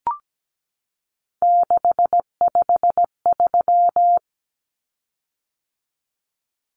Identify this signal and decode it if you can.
Morse code, heard as a keyed tone, about 17 wpm, 700 Hz. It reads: 653
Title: Common Cut Numbers - 3-Digit 17wpm